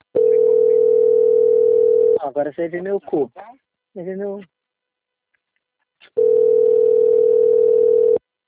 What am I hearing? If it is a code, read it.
Telephone call progress tone (North America), heard as a telephone network's tone, about 440, 480 Hz, with speech. Ringback tone.